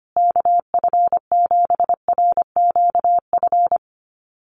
Morse code, heard as a keyed tone, about 25 wpm, 700 Hz. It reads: KF7RQF
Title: USA Call Signs Repeated 2x 25wpm